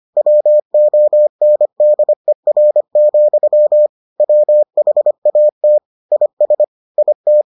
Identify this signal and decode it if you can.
Morse code, heard as a keyed tone, about 25 wpm, 600 Hz. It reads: WONDER, WHAT IS IT